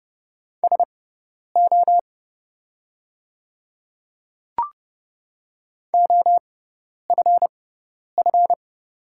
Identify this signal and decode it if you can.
Morse code, heard as a keyed tone, about 30 wpm, spaced as for 10 wpm, 690 Hz. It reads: SO OFF
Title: 6x Character Spacing - Top 200 Words - 30wpm